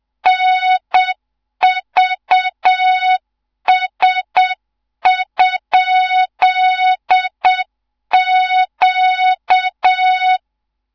Morse code, heard as a keyed tone, about 7 wpm, 3700 Hz, with noise. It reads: NVS?Q
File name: Morze-HA2MN / hang / 14 280 HA2MN